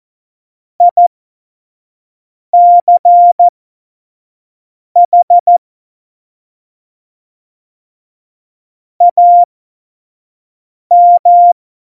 Morse code, heard as a keyed tone, about 14 wpm, spaced as for 5 wpm, 700 Hz. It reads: ICH AM
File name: Morsekurs / Witze / Witz 06 March 2018, 25 F70